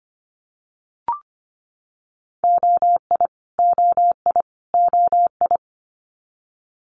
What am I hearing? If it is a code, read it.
Morse code, heard as a keyed tone, about 25 wpm, 700 Hz. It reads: OI OI OI
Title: Multi-letter Phonemes Repeated 3x 25wpm